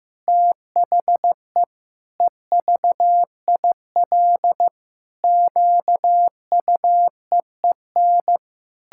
Morse code, heard as a keyed tone, about 16 wpm, 700 Hz. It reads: THE EVIL QUEEN
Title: Pangram Challenge v2 - 15wpm